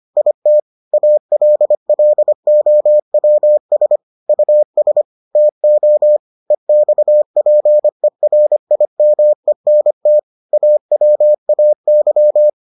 Morse code, heard as a keyed tone, about 25 wpm, 600 Hz. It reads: IT ALLOWS US TO EXPERIMENT AWAY